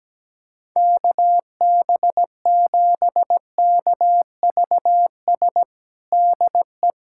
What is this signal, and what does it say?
Morse code, heard as a keyed tone, about 17 wpm, 700 Hz. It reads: KB7KVS DE
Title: USA Callsign DE Callsign 17wpm